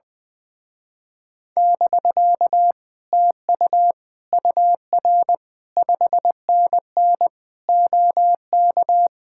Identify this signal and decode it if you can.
Morse code, heard as a keyed tone, about 19 wpm, 700 Hz. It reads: <BK> TU UR 5NN OK
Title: Speed-Racing Common QSO Phrases - 20wpm